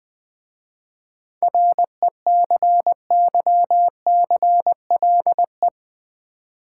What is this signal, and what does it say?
Morse code, heard as a keyed tone, about 20 wpm, 700 Hz. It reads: RECYCLE
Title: 7 Letter Words - First 3 Letters Given - 20wpm